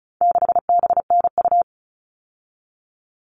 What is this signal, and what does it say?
Morse code, heard as a keyed tone, about 34 wpm, 700 Hz. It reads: 6BNU
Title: All Cut Numbers - 4-Digit 35wpm